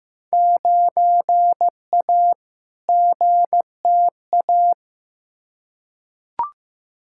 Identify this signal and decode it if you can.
Morse code, heard as a keyed tone, about 15 wpm, 700 Hz. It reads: 9A GTA
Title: Field Day Exchanges Encore - 15wpm